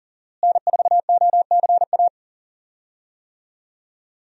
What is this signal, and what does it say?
Morse code, heard as a keyed tone, about 40 wpm, 700 Hz. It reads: N4OCA
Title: USA Call Signs - 5 Characters - Repeated 3x - 40wpm